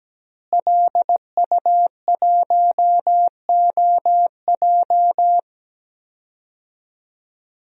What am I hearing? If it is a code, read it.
Morse code, heard as a keyed tone, about 17 wpm, 700 Hz. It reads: LU1OJ
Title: DXCC-100 Call Signs - 5 Characters - Repeated 2x - 17wpm